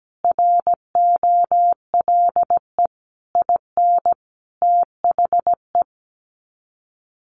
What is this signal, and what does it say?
Morse code, heard as a keyed tone, about 17 wpm, 700 Hz. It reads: ROLE IN THE
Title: Sets of 5 Words Encore2 17wpm